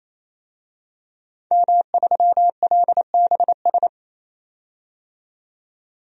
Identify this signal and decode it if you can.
Morse code, heard as a keyed tone, about 28 wpm, 700 Hz. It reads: M3LBS
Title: DXCC-100 Call Signs - 5 Characters - 28wpm